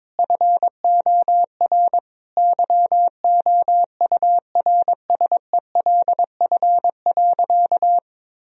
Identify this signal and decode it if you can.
Morse code, heard as a keyed tone, about 22 wpm, 700 Hz. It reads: FOR YOURSELF.
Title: Sentences from Top 1000 Words - Review Part 4 - 22wpm